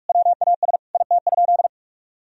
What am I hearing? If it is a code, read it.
Morse code, heard as a keyed tone, about 42 wpm, 700 Hz. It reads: WAS IT?
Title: Sentences from Top 1000 Words - Review Part 2 - 45wpm